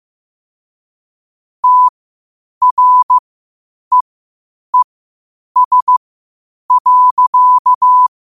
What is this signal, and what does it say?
Morse code, heard as a keyed tone, about 15 wpm, spaced as for 8 wpm, 1000 Hz. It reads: TREES.
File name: qotdfiles / 8wpm